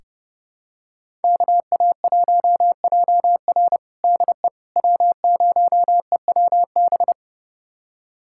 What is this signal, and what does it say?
Morse code, heard as a keyed tone, about 30 wpm, 700 Hz. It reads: KA1JR DE W0EWB
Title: USA Callsign DE Callsign - 5 Characters - 30wpm